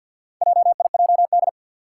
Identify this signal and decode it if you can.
Morse code, heard as a keyed tone, about 47 wpm, 700 Hz. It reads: JIJD
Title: Four Challenging Letter-Number Combinations 50wpm